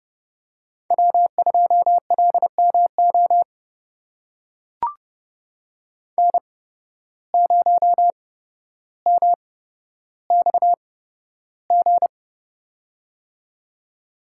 Morse code, heard as a keyed tone, about 30 wpm, spaced as for 13 wpm, 700 Hz. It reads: W2LMO N0MXG
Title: USA Call Signs - 5 Characters - 8x Character Spacing - 30wpm